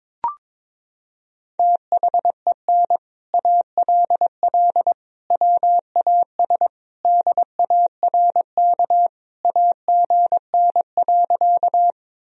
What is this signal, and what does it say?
Morse code, heard as a keyed tone, about 22 wpm, 700 Hz. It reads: THEN ALL WAS DARK AGN.
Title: Sentences from Top 500 Words and Abbreviations 22wpm